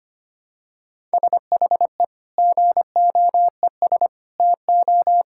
Morse code, heard as a keyed tone, about 25 wpm, 700 Hz. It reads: SHE GOES TO
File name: practice / 25wpm / Sentences with 2-Word Phrases - 25wpm